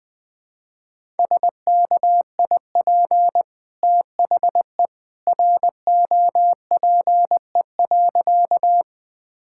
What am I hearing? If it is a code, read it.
Morse code, heard as a keyed tone, about 20 wpm, 700 Hz. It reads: SKIP THE ROPE.
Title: Easy Sentences with Hints - 20wpm